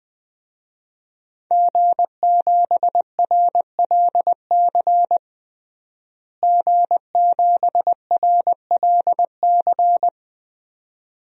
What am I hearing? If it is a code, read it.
Morse code, heard as a keyed tone, about 20 wpm, 700 Hz. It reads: G7RLC G7RLC
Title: DXCC-100 Call Signs - 5 Characters - Repeated 2x - 20wpm